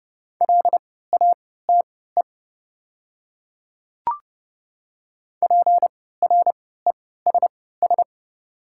Morse code, heard as a keyed tone, about 30 wpm, spaced as for 17 wpm, 690 Hz. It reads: LATE PRESS
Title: 3x Character Spacing - Top 300 Words - 30wpm